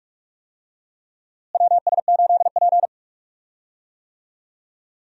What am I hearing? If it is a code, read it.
Morse code, heard as a keyed tone, about 45 wpm, 700 Hz. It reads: WS8P
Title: USA Call Signs - 4 Characters - Repeated 2x - 45wpm